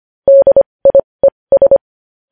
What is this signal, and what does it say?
Morse code, heard as a keyed tone, about 25 wpm, spaced as for 20 wpm, 550 Hz. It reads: DIES
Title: News Headlines at 20